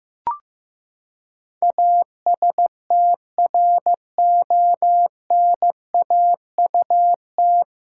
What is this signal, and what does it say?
Morse code, heard as a keyed tone, about 15 wpm, 700 Hz. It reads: ASTRONAUT
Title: Pangram Challenge v2 - 15wpm